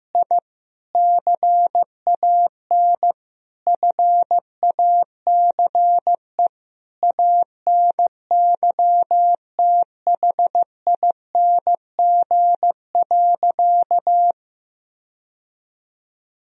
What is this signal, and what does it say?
Morse code, heard as a keyed tone, about 15 wpm, 700 Hz. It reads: I CAN FACE ANYTHING.
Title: Sentences from Top 1000 Words - Review Part 2 - 15wpm